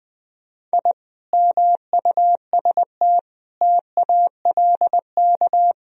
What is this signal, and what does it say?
Morse code, heard as a keyed tone, about 20 wpm, 700 Hz. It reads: I MUST TALK
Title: Sentences from Top 1000 Words - Review Part 4 - 20wpm